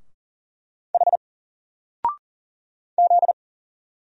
Morse code, heard as a keyed tone, about 40 wpm, 700 Hz. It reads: H Z
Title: Single Letter-Number Alt 40wpm